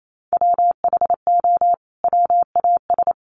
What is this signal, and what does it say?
Morse code, heard as a keyed tone, about 28 wpm, 700 Hz. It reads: WHO WAS
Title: Sentences from Top 100 Words 28wpm